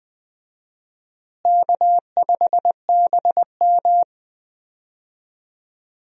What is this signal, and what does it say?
Morse code, heard as a keyed tone, about 20 wpm, 700 Hz. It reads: K5BM
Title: USA Call Signs - 4 Characters - Repeated 3x - 20wpm